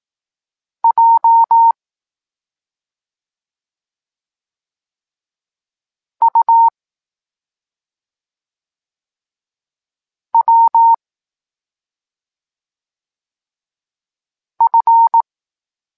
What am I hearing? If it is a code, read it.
Morse code, heard as a keyed tone, about 18 wpm, 920 Hz. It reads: J U W F
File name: Tekst-26 3